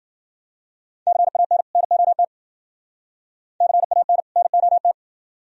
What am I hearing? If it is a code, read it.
Morse code, heard as a keyed tone, about 50 wpm, 700 Hz. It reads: CAN NOT CAN NOT